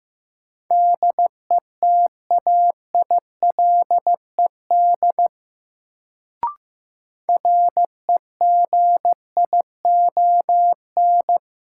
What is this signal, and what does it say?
Morse code, heard as a keyed tone, about 15 wpm, 700 Hz. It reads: DETAILED REGION
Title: Sentences from Top 1500 Words - New Word Drill Alt - 15wpm